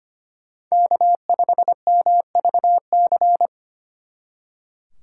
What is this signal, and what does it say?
Morse code, heard as a keyed tone, about 30 wpm, 700 Hz. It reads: K5MVC
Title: Speed Racing - USA Call Signs 25wpm